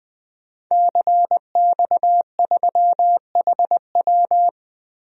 Morse code, heard as a keyed tone, about 20 wpm, 700 Hz. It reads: CX3HW
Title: DXCC-100 Call Signs - 5 Characters - Repeated 3x - 20wpm